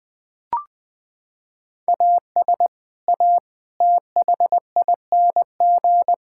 Morse code, heard as a keyed tone, about 21 wpm, 700 Hz. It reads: AS A THING
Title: Sentences from Top 1000 Words - Review Part 6 - 20wpm